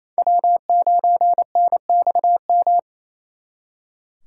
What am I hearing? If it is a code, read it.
Morse code, heard as a keyed tone, about 37 wpm, 700 Hz. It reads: W9NXM